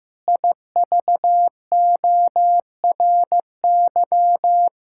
Morse code, heard as a keyed tone, about 15 wpm, 700 Hz. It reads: IVORY
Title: Pangram Challenge v2 - 15wpm